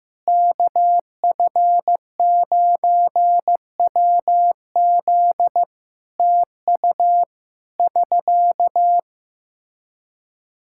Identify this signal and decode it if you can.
Morse code, heard as a keyed tone, about 15 wpm, 700 Hz. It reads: KF9WZ TU <SK>
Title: K1USN SST QSO Practice 15wpm